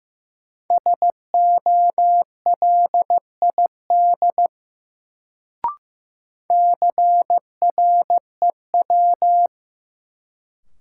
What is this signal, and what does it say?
Morse code, heard as a keyed tone, about 15 wpm, 700 Hz. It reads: SOLID CREW
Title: Sentences from Top 1500 Words - New Word Drill Alt - 15wpm